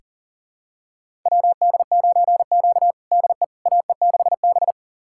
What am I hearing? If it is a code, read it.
Morse code, heard as a keyed tone, about 40 wpm, 700 Hz. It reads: WD9Q DE AE6B
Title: USA Callsign DE Callsign - 4 Characters - 40wpm